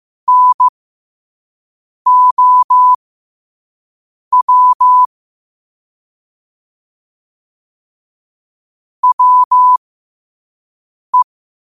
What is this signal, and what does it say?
Morse code, heard as a keyed tone, about 15 wpm, spaced as for 5 wpm, 1000 Hz. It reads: NOW WE
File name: 5wpm